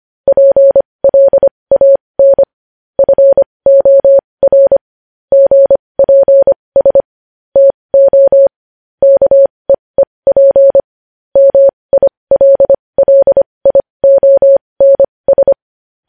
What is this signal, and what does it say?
Morse code, heard as a keyed tone, about 25 wpm, spaced as for 20 wpm, 550 Hz. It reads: PLAN FOR GPS TO KEEP MILLIONS
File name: podcast / News headlines-1x-25x20